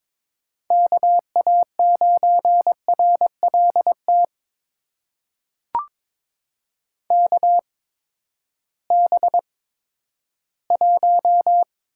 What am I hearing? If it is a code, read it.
Morse code, heard as a keyed tone, about 22 wpm, spaced as for 12 wpm, 700 Hz. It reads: KA9RLT KB1
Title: USA Call Signs - 6 Characters - 8x Character Spacing - 22wpm